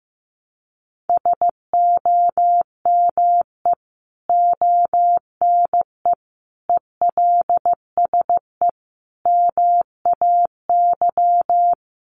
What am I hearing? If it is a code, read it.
Morse code, heard as a keyed tone, about 14 wpm, 700 Hz. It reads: SOME ONE ELSE MAY
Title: Sentences from Top 1100 Words 15wpm